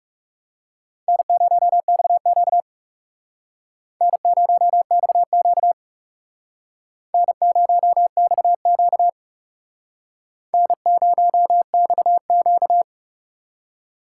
Morse code, heard as a keyed tone, about 35 wpm, 700 Hz. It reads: N0XQ N0XQ N0XQ N0XQ
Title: Speed Racing - USA Call Signs 30wpm